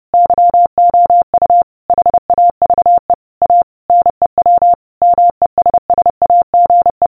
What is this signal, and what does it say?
Morse code, heard as a keyed tone, about 30 wpm, 700 Hz. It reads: YOU HAVE A NEW MESSAGE